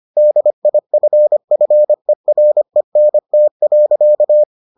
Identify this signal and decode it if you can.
Morse code, heard as a keyed tone, about 25 wpm, 600 Hz. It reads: DIFFERENT.